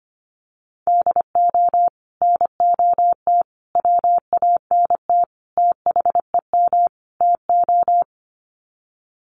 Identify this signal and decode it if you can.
Morse code, heard as a keyed tone, about 25 wpm, 690 Hz. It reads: DO NOT WANT THEM TO
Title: Sets of 5 Words Encore 25wpm